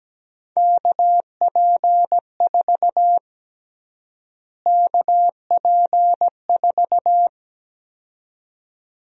Morse code, heard as a keyed tone, about 17 wpm, 700 Hz. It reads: KP4 KP4
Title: USA Call Sign Prefixes - Repeated 2x - 17wpm